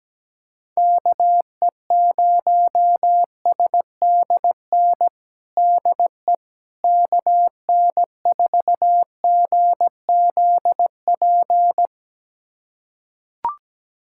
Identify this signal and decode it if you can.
Morse code, heard as a keyed tone, about 17 wpm, 700 Hz. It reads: KE0SDN DE KN4GZP
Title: USA Callsign DE Callsign - 6 Characters - 17wpm